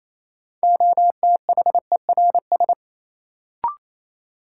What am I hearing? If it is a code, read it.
Morse code, heard as a keyed tone, about 28 wpm, 700 Hz. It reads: OTHERS
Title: Sentences from Top 1000 Words - Review Part 2 - New Word Drill - 28wpm